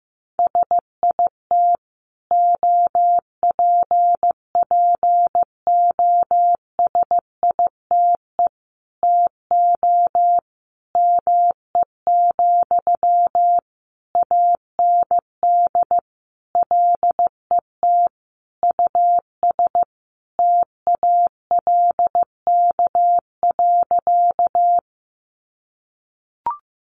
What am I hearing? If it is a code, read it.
Morse code, heard as a keyed tone, about 15 wpm, 700 Hz. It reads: SIT OPPOSITE TO ME, AND LET US TALK.
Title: Sentences from Top 1300 Words 15wpm